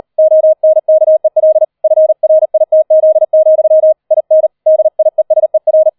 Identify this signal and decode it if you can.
Morse code, heard as a keyed tone, about 38 wpm, 630 Hz. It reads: ONKEL FRITZ, IN DIESER
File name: 40wpm